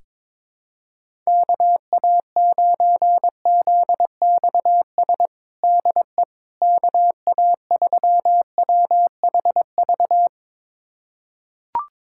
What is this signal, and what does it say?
Morse code, heard as a keyed tone, about 22 wpm, 700 Hz. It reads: KA9ZXS DE KA3WHV
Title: USA Callsign DE Callsign - 6 Characters - 22wpm